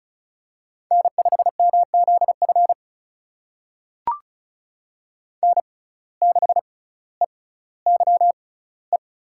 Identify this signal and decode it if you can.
Morse code, heard as a keyed tone, about 36 wpm, spaced as for 18 wpm, 700 Hz. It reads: N5MZF N6EYE
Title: USA Call Signs - 5 Characters - 6x Character Spacing - 35wpm